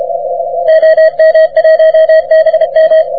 Morse code, heard as a keyed tone, about 32 wpm, 640 Hz. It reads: OM1BM
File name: wp / k5d 30m cw